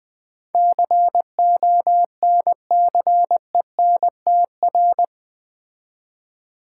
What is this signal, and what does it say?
Morse code, heard as a keyed tone, about 20 wpm, 700 Hz. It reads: CONCENTR
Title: Build A Word - 4 Letters At A Time - 11-Letter Words - 20wpm